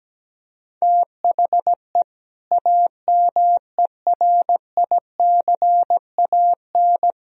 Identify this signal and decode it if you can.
Morse code, heard as a keyed tone, about 17 wpm, 700 Hz. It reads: THE AMERICAN
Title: Speed-Racing - Sets of 2 Words 17wpm